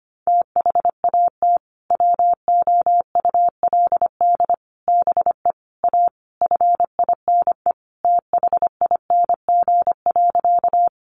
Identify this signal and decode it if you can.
Morse code, heard as a keyed tone, about 25 wpm, 700 Hz. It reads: THAT WOULD BE A FINE THING.
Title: Sentences from Top 500 Words 25wpm